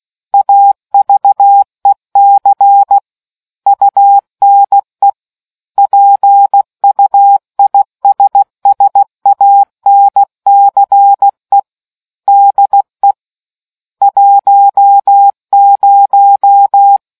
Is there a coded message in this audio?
Yes: Morse code from 0.3 to 17.0 s, heard as a keyed tone, about 16 wpm, 800 Hz. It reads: AVEC UNE PUISSANCE DE 10